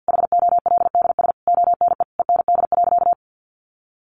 Morse code, heard as a keyed tone, about 50 wpm, 700 Hz. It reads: HOLDS ONE END.
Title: Easy Phrases 50wpm